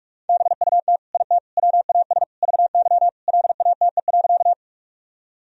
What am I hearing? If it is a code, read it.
Morse code, heard as a keyed tone, about 46 wpm, 700 Hz. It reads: BUT IT WAS VY LATE.
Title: Sentences from Top 300 Words and Abbreviations 45wpm